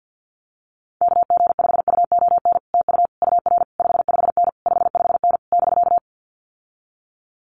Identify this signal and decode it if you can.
Morse code, heard as a keyed tone, about 50 wpm, 700 Hz. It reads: KG5UON TU UR 55N 55N <BK>
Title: Common QSO Phrases Encore 50wpm